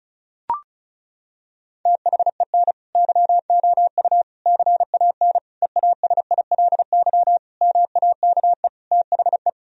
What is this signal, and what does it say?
Morse code, heard as a keyed tone, about 36 wpm, 700 Hz. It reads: THEN YOU CAN EASILY MAKE THE
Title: Sentences from Top 1000 Words - Review Part 1 - 35wpm